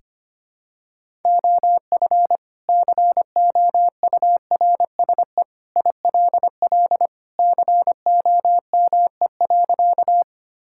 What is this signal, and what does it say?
Morse code, heard as a keyed tone, about 25 wpm, 700 Hz. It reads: OF COURSE ILL COME.
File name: practice / 25wpm / Sentences with 2-Word Phrases - 25wpm